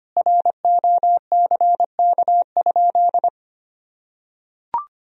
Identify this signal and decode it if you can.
Morse code, heard as a keyed tone, about 25 wpm, 700 Hz. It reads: ROCK?
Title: Sentences from Top 1000 Words - Review Part 2 - 25wpm